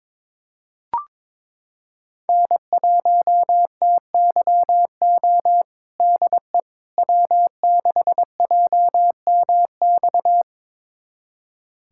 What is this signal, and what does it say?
Morse code, heard as a keyed tone, about 22 wpm, 700 Hz. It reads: N1TYO DE W6JMX
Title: USA Callsign DE Callsign - 5 Characters - 22wpm